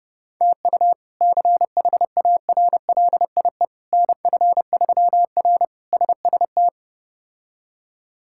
Morse code, heard as a keyed tone, about 30 wpm, 700 Hz. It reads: TU CHARLIE NF3R SST
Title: K1USN SST QSO Practice - 30wpm